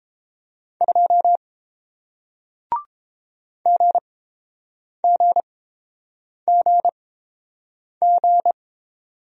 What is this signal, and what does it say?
Morse code, heard as a keyed tone, about 30 wpm, spaced as for 7 wpm, 700 Hz. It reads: 2 GGGG